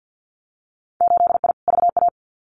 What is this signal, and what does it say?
Morse code, heard as a keyed tone, about 50 wpm, 700 Hz. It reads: 8I VA
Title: Winter Field Day Exchanges Encore 50wpm